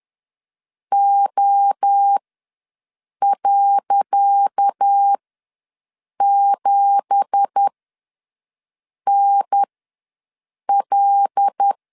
Morse code, heard as a keyed tone, about 10 wpm, spaced as for 6 wpm, 800 Hz. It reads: O.7NL
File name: LEZIONI / 10 SIGNS / 10C DOT ws 10wpm 4213